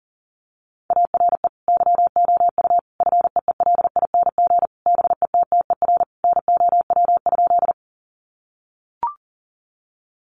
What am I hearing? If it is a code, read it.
Morse code, heard as a keyed tone, about 40 wpm, 700 Hz. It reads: ARE YOU FEELING BETTER NOW?
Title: Sentences from Top 1200 Words 40wpm